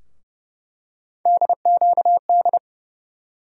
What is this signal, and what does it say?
Morse code, heard as a keyed tone, about 30 wpm, 700 Hz. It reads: DQD
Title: USA Call Sign Postfixes - 3 Letters 30wpm